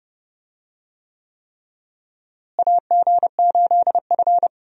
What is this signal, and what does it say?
Morse code, heard as a keyed tone, about 30 wpm, 700 Hz. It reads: AG8F
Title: USA Call Signs - 4 Characters - Repeated 3x - 30wpm